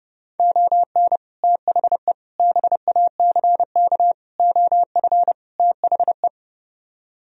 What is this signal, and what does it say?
Morse code, heard as a keyed tone, about 37 wpm, 700 Hz. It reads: ON THE BACK OF THE
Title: Speed-Racing - Sets of 5 Words 30wpm